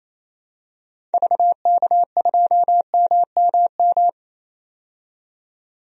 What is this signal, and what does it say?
Morse code, heard as a keyed tone, about 28 wpm, 700 Hz. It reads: VK2MMM
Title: DXCC-100 Call Signs - 6 Characters - 28wpm